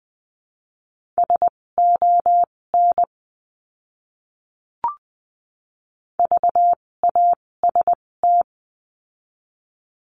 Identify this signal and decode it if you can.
Morse code, heard as a keyed tone, about 20 wpm, spaced as for 16 wpm, 700 Hz. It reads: SON VAST
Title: 256 Lesson 42a - Words 12-20wpm